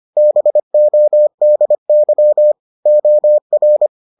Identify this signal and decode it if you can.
Morse code, heard as a keyed tone, about 25 wpm, 600 Hz. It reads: BODY OR